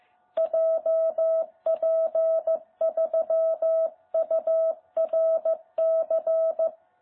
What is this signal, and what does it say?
Morse code, heard as a keyed tone, about 14 wpm, 660 Hz. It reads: JP3URC
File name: jh8cbh / 2021cw